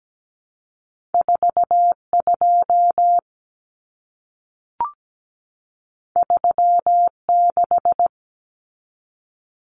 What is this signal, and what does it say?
Morse code, heard as a keyed tone, about 17 wpm, 700 Hz. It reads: 42 36
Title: Common Cut Numbers - 2-Digit 17wpm